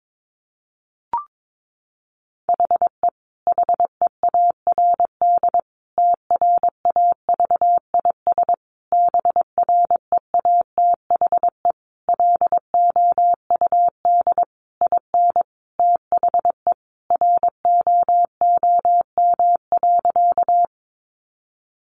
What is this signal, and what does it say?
Morse code, heard as a keyed tone, about 22 wpm, 700 Hz. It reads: HE HEARD TRAVIS BREATHE LOUD IN THE ROOM.
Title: A Sound Of Thunder 22wpm